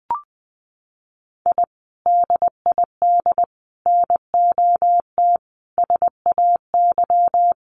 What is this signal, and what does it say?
Morse code, heard as a keyed tone, about 20 wpm, 700 Hz. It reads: I DID NOT SAY
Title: Sets of 5 Words Encore2 20wpm